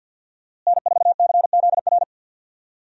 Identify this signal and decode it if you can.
Morse code, heard as a keyed tone, about 48 wpm, 700 Hz. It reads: N4CZR
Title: USA Call Signs - 5 Characters - Repeated 2x - 50wpm